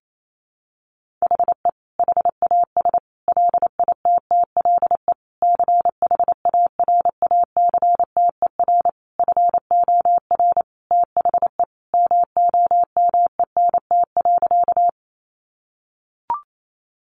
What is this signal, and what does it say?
Morse code, heard as a keyed tone, about 28 wpm, 700 Hz. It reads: HE HAS LITTLE CHARACTER FOR THE MOMENT.
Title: Sentences from Top 1200 Words 28wpm